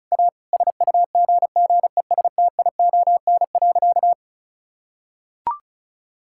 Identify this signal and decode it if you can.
Morse code, heard as a keyed tone, about 34 wpm, 700 Hz. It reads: A SUGGESTION.
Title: Sentences from Top 1600 Words 35wpm